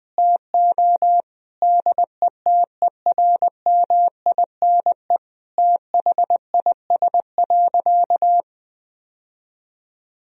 Morse code, heard as a keyed tone, about 21 wpm, 700 Hz. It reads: TO DETERMINE THIS.
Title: Sentences from Top 1000 Words - Review Part 4 - 20wpm